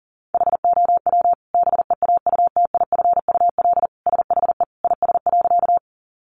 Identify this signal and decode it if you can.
Morse code, heard as a keyed tone, about 40 wpm, 700 Hz. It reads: HOW BEAUTIFUL SHE IS.
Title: Sentences from Top 900 Words 40wpm